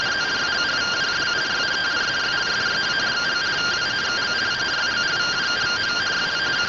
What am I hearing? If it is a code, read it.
Two-tone teleprinter signal, about 170 Hz shift.